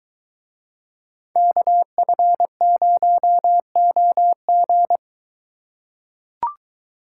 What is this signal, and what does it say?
Morse code, heard as a keyed tone, about 22 wpm, 700 Hz. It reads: KF0OG